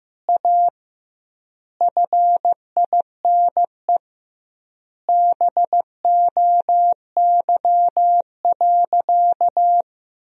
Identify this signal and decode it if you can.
Morse code, heard as a keyed tone, about 16 wpm, 700 Hz. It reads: A FINE BOY.
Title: 2x Word-Spacing - Sentences from Top 500 Words - 15wpm